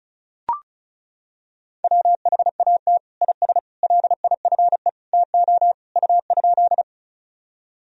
Morse code, heard as a keyed tone, about 36 wpm, 700 Hz. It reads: WHAT IS LIFE TO U?
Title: Sentences from Top 300 Words and Abbreviations 35wpm